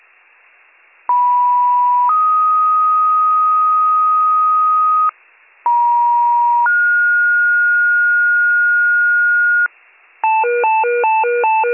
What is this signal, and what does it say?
Two-tone page, heard as a one-second tone, then a three-second tone, about 980 then 1250 Hz, about 950 then 1430 Hz, with noise. It is heard from 1.1 to 9.7 s.